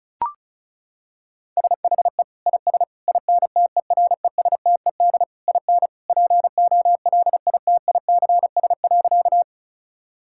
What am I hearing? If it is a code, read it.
Morse code, heard as a keyed tone, about 36 wpm, 700 Hz. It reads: SHE IS INTERESTED IN POLITICS.
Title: Sentences from Top 1000 Words - Review Part 6 - 35wpm